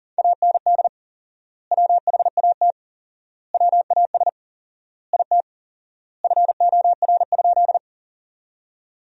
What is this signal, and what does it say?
Morse code, heard as a keyed tone, about 40 wpm, 700 Hz. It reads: AND WHAT WAS IT FOR?
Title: 4x Word-Spacing - Sentences from Top 100 Words - 40wpm